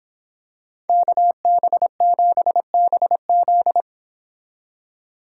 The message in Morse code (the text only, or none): KB7BZ